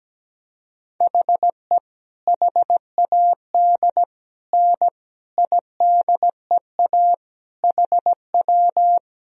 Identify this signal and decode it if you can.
Morse code, heard as a keyed tone, about 17 wpm, 700 Hz. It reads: HE HAD N IDEA HW
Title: Sentences from Top 300 Words and Abbreviations 17wpm